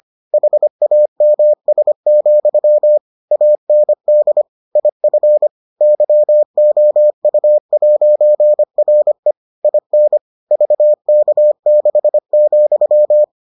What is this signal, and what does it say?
Morse code, heard as a keyed tone, about 25 wpm, 600 Hz. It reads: HAMS, AND IF YOU'RE IN VK6,